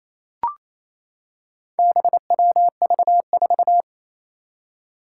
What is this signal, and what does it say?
Morse code, heard as a keyed tone, about 28 wpm, 700 Hz. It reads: BWV4
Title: Four Challenging Letter-Number Combinations 28wpm